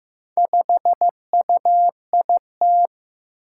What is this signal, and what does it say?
Morse code, heard as a keyed tone, about 15 wpm, 700 Hz. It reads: 5UIT
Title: Four Challenging Letter-Number Combinations 15wpm